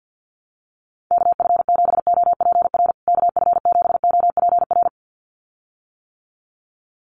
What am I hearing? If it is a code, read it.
Morse code, heard as a keyed tone, about 49 wpm, 700 Hz. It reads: KF7OPR KF7OPR
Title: USA Call Signs Repeated 2x 50wpm